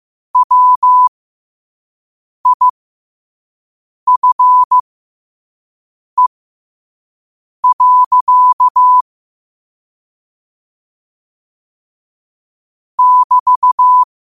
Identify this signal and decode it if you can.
Morse code, heard as a keyed tone, about 15 wpm, spaced as for 5 wpm, 1000 Hz. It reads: WIFE. <BT>